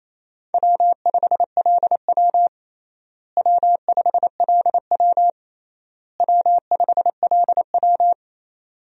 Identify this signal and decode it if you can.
Morse code, heard as a keyed tone, about 28 wpm, 700 Hz. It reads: W5LW W5LW W5LW
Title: USA Call Signs - 4 Characters - Repeated 3x - 28wpm